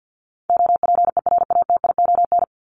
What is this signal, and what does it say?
Morse code, heard as a keyed tone, about 48 wpm, 700 Hz. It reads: OPERATION
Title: Top 1501-1750 Words 50wpm